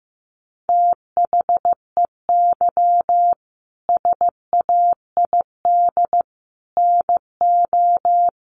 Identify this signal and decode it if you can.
Morse code, heard as a keyed tone, about 15 wpm, 700 Hz. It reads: THEY SAID NO
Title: Sentences from Top 100 Words 15wpm